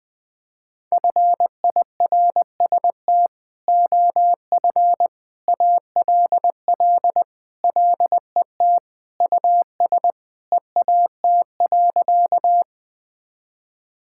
Morse code, heard as a keyed tone, about 21 wpm, 700 Hz. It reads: FIRST OF ALL LET US EAT.
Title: Sentences with 3-Word Phrases - 20wpm